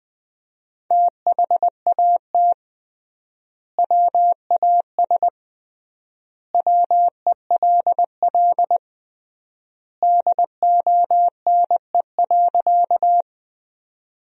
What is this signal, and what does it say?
Morse code, heard as a keyed tone, about 20 wpm, 700 Hz. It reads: THAT WAS WELL DONE.
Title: 3x Word-Spacing - Sentences from Top 500 Words - 20wpm